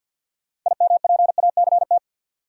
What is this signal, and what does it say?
Morse code, heard as a keyed tone, about 49 wpm, 700 Hz. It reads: IMPACT
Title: Sentences from Top 1000 Words - Review Part 6 - New Word Drill - 50wpm